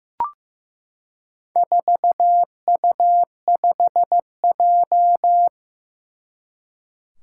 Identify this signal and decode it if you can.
Morse code, heard as a keyed tone, about 15 wpm, 700 Hz. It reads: 4U5J
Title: Four Challenging Letter-Number Combinations 15wpm